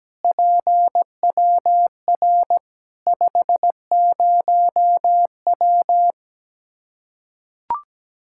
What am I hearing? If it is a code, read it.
Morse code, heard as a keyed tone, about 17 wpm, 700 Hz. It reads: PWR 50W
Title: Practice QSO - Exchange 2 - Info - 17wpm